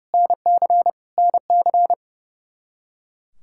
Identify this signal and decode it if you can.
Morse code, heard as a keyed tone, about 30 wpm, 700 Hz. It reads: NC NC
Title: K1USN SST QSO Practice 30wpm